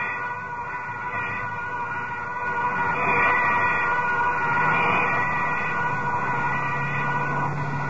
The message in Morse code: none